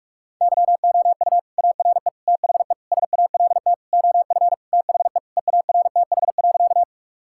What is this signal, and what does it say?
Morse code, heard as a keyed tone, about 46 wpm, 700 Hz. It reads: YOU ARE THE SALT OF THE EARTH.
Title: Sentences from Top 1000 Words - Review Part 5 - 45wpm